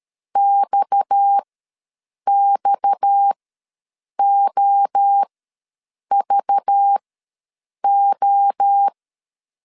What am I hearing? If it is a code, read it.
Morse code, heard as a keyed tone, about 13 wpm, spaced as for 7 wpm, 800 Hz. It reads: XXOVO
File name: LEZIONI / 6 PKX0 / 6 PKX ws 12wpm 3106